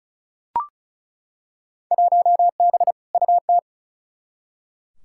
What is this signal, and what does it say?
Morse code, heard as a keyed tone, about 36 wpm, 700 Hz. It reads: 1B UT
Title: Field Day Exchanges - 35wpm